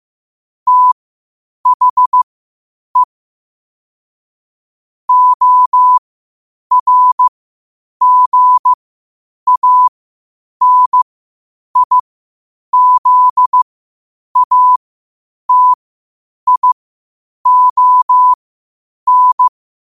Morse code, heard as a keyed tone, about 15 wpm, spaced as for 8 wpm, 1000 Hz. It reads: THE ORGANIZATION
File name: qotdfiles / 8wpm